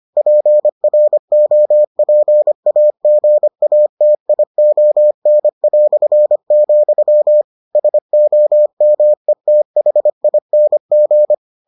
Morse code, heard as a keyed tone, about 25 wpm, 600 Hz. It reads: PROPAGATION", SOMETHING